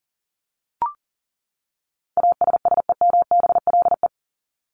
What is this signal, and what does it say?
Morse code, heard as a keyed tone, about 40 wpm, 700 Hz. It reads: ASSEMBLE
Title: LY Suffix - Top 100 Words with Base Words 40wpm